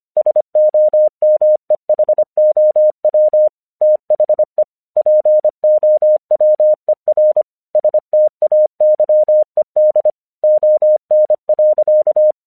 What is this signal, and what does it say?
Morse code, heard as a keyed tone, about 24 wpm, 610 Hz. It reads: SOMEHOW THE POWER STAYED ON.
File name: 1 English 25wpm / AQC18